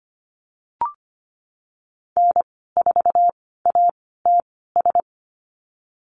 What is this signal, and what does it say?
Morse code, heard as a keyed tone, about 25 wpm, spaced as for 16 wpm, 700 Hz. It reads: N4ATS
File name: Lesson 15g - Number 5 - Call Signs 25wpm